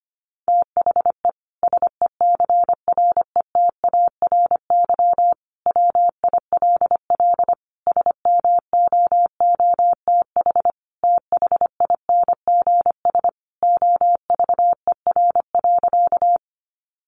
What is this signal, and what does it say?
Morse code, heard as a keyed tone, about 25 wpm, 700 Hz. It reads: THE SECRETARY WILL SMOOTH THINGS OVER.
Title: Sentences from Top 1400 Words 25wpm